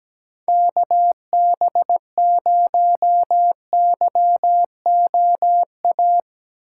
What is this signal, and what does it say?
Morse code, heard as a keyed tone, about 17 wpm, 700 Hz. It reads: KB0YOA
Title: USA Callsign DE Callsign - 6 Characters - 17wpm